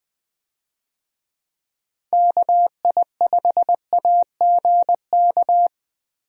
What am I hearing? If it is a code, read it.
Morse code, heard as a keyed tone, about 20 wpm, 700 Hz. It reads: KI5AGK
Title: USA Call Signs - 6 Characters - Repeated 2x - 20wpm